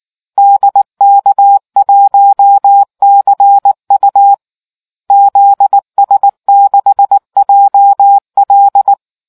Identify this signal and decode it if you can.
Morse code, heard as a keyed tone, about 19 wpm, 800 Hz. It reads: DK1CU ZS6JL